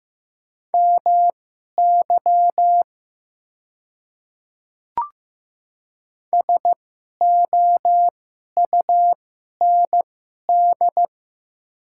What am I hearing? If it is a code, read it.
Morse code, heard as a keyed tone, about 15 wpm, spaced as for 11 wpm, 700 Hz. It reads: MY SOUND
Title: 2x Character Spacing - Top 200 Words - 15wpm